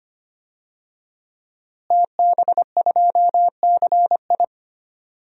Morse code, heard as a keyed tone, about 25 wpm, 700 Hz. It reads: TB2CI
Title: DXCC-100 Call Signs - 5 Characters - Repeated 2x - 25wpm